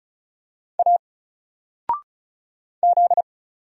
Morse code, heard as a keyed tone, about 36 wpm, 700 Hz. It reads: A Z
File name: Single-Letter-Number-35wpm